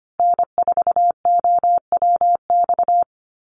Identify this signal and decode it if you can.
Morse code, heard as a keyed tone, about 25 wpm, 700 Hz. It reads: N4OWX
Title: USA Call Signs Repeated 3x 25wpm